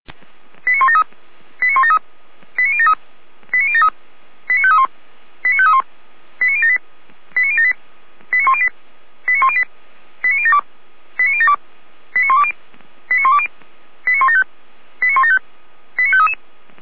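Five-tone selective call (ZVEI), heard as quick runs of five tones, tones of about 70 ms. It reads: 89173 89173 89073 89073 89531 89531 89078 89078 89108 89108 89072 89072 89120 89120 89175 89175 89533
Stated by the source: ZVEI five-tone call received on SDR.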